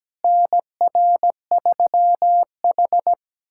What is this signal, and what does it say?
Morse code, heard as a keyed tone, about 17 wpm, 700 Hz. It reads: NR3H
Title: USA Callsign DE Callsign - 4 Characters - 17wpm